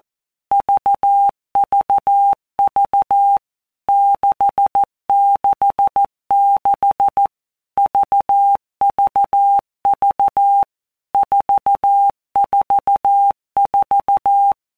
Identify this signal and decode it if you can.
Morse code, heard as a keyed tone, about 14 wpm, 800 Hz. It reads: VVV 666 VVV 444
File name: Basic Letter Groups By Blocks / Groups@13 / VB46@13